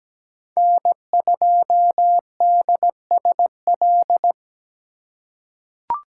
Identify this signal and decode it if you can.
Morse code, heard as a keyed tone, about 17 wpm, 700 Hz. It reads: N2DSL
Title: USA Call Signs - 5 Characters - 3x Character Spacing - 17wpm